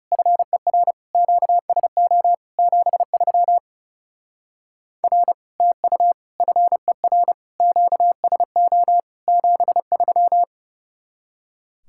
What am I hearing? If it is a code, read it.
Morse code, heard as a keyed tone, about 35 wpm, 700 Hz. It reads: FER QSO 73 R TU FER QSO 73
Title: Speed-Racing Common QSO Phrases - 30wpm